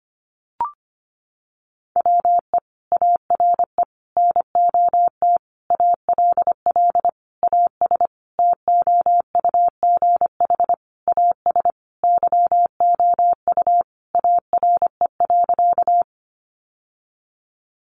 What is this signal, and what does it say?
Morse code, heard as a keyed tone, about 25 wpm, 700 Hz. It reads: WE ARE NOT ALL AS TOUGH AS YOU ARE.
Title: Sentences from Top 1200 Words 25wpm